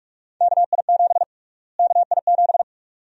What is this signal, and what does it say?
Morse code, heard as a keyed tone, about 46 wpm, 700 Hz. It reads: KI7 KI7
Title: USA Call Sign Prefixes - Repeated 2x - 45wpm